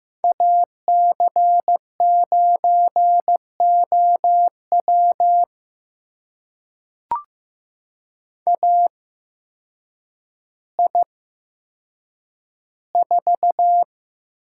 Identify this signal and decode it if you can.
Morse code, heard as a keyed tone, about 15 wpm, spaced as for 8 wpm, 700 Hz. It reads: AC9OW AI4
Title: USA Call Signs - 5 Characters - 8x Character Spacing - 15wpm